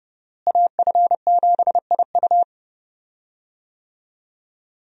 Morse code, heard as a keyed tone, about 30 wpm, 700 Hz. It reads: AF7IU
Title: USA Call Signs - 5 Characters - Repeated 2x - 30wpm